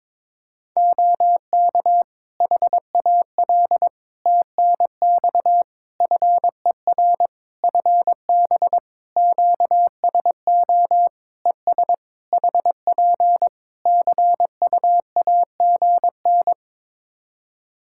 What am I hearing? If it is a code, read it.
Morse code, heard as a keyed tone, about 22 wpm, 700 Hz. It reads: OK HAL TNX FER FB QSO ES HP CUAGN
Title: Practice QSO - Exchange 4 - Ending - 22wpm